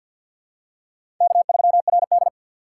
Morse code, heard as a keyed tone, about 52 wpm, 700 Hz. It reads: K3RD
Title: USA Call Signs - 4 Characters - 3x Character Spacing - 50wpm